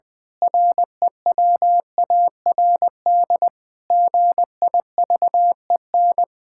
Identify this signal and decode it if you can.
Morse code, heard as a keyed tone, about 20 wpm, 700 Hz. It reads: REWARD GIVEN
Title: Pangram Challenge v2 - 20wpm